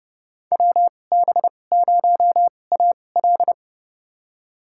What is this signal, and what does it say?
Morse code, heard as a keyed tone, about 32 wpm, spaced as for 22 wpm, 700 Hz. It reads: WB0AL
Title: USA Call Signs - 5 Characters - 2x Character Spacing - 30wpm